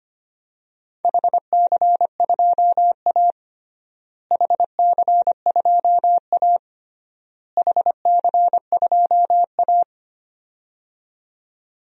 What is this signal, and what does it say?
Morse code, heard as a keyed tone, about 25 wpm, 700 Hz. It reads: HC2A HC2A HC2A
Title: DXCC-100 Call Signs - 4 Characters - Repeated 3x - 25wpm